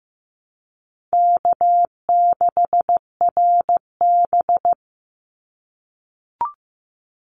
Morse code, heard as a keyed tone, about 15 wpm, 700 Hz. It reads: K6RB
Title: Easy Call Signs - 15wpm